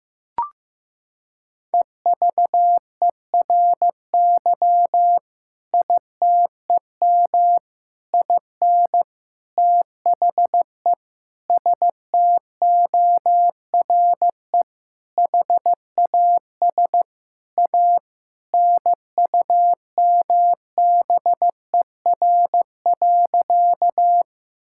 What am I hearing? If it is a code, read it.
Morse code, heard as a keyed tone, about 15 wpm, 700 Hz. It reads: EVERY ITEM IN THE STORE HAS A NUMBER.
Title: Sentences from Top 1000 Words - Review Part 6 - 15wpm